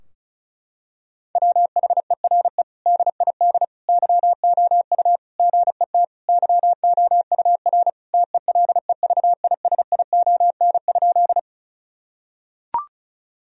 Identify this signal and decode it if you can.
Morse code, heard as a keyed tone, about 36 wpm, 700 Hz. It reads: WHERE DID YOU GET YOUR TELEVISION?
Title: Sentences from Top 1000 Words - Review Part 4 - 35wpm